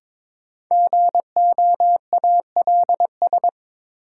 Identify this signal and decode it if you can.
Morse code, heard as a keyed tone, about 21 wpm, 700 Hz. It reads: GOALS
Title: Sentences from Top 1000 Words - Review Part 6 - New Word Drill - 22wpm